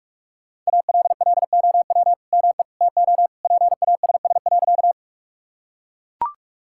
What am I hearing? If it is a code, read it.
Morse code, heard as a keyed tone, about 44 wpm, 700 Hz. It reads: ALLOW ME TO PASS.
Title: Sentences from Top 1000 Words - Review Part 5 - 45wpm